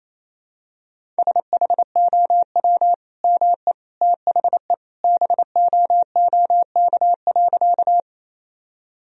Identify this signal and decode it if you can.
Morse code, heard as a keyed tone, about 27 wpm, 700 Hz. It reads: SHOW ME THE BOOK.
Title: Sentences from Top 1000 Words - Review Part 1 - 28wpm